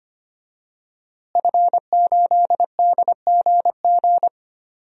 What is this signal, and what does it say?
Morse code, heard as a keyed tone, about 25 wpm, 700 Hz. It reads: F8DGG
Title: DXCC-100 Call Signs - 5 Characters - Repeated 3x - 25wpm